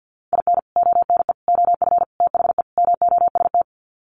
Morse code, heard as a keyed tone, about 50 wpm, 700 Hz. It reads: IN ONE OF THE MOST
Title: Sets of 5 Words Encore 50wpm